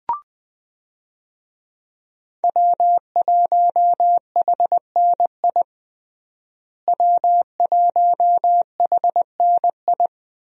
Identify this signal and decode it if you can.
Morse code, heard as a keyed tone, about 21 wpm, 700 Hz. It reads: W1HNI W1HNI
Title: USA Call Signs - 5 Characters - Repeated 2x - 20wpm